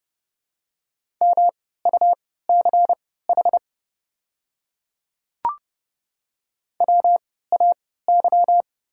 Morse code, heard as a keyed tone, about 30 wpm, spaced as for 17 wpm, 690 Hz. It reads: MUCH WAY
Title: 3x Character Spacing - Top 300 Words - 30wpm